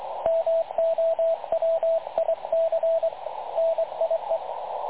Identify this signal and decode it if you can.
Morse code, heard as a keyed tone, about 22 wpm, 660 Hz, with radio noise. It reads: MOWIC NIE
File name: dlazaawansowanych / nagraniazpasma / websdr recording 3567.0kHz